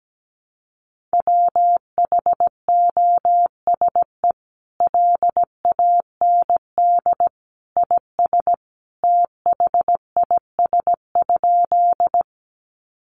Morse code, heard as a keyed tone, about 17 wpm, 700 Hz. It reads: WHOSE LAND IS THIS?
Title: Sentences from Top 1200 Words 17wpm